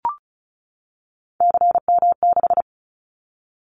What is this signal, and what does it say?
Morse code, heard as a keyed tone, about 36 wpm, 700 Hz. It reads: CM6
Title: Callsign Prefixes for Top 100 DXCC - 35wpm